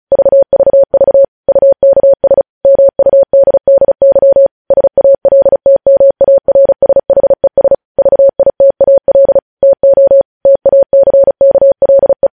Morse code, heard as a keyed tone, about 36 wpm, 550 Hz. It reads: VVV UKS MUDDY SALTMARSHES VITAL TO TACKLE
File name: podcast / News headlines-1x-35x35